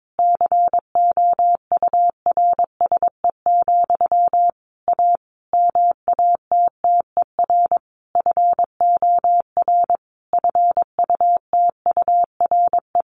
Morse code, heard as a keyed tone, about 22 wpm, 700 Hz. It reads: COURSE, A MATTER FOR FUTURE
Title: Sentences from Top 1500 Words 22wpm